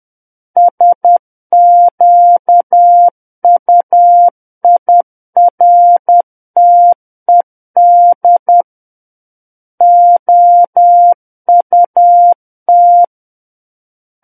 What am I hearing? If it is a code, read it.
Morse code, heard as a keyed tone, about 10 wpm, 700 Hz. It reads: SQUIRTED OUT
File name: Book One / 01-Chapter 1 - 10 WPM